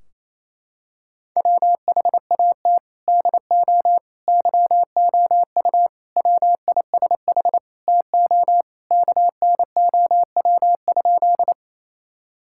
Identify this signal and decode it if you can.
Morse code, heard as a keyed tone, about 28 wpm, 700 Hz. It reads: WHAT DO YOU WISH TO KNOW?
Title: Sentences from Top 1600 Words 28wpm